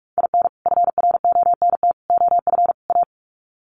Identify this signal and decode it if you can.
Morse code, heard as a keyed tone, about 44 wpm, 700 Hz. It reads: IN FRONT OF A
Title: Sets of 4 Words 45wpm